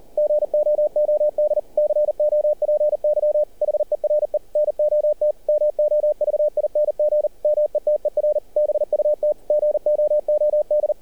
Morse code, heard as a keyed tone, about 38 wpm, 600 Hz, with radio noise. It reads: GOOD COPY HERE NOT MOVING METER BUT GOOD
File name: archived / CRSnet